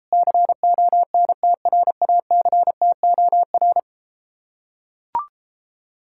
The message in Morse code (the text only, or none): CONTRACTOR